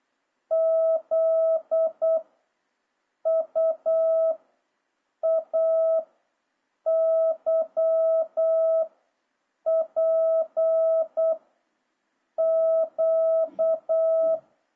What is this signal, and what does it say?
Morse code, heard as a keyed tone, about 8 wpm, 650 Hz. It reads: Z U A Y P Q
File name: jh8cbh / 2026cw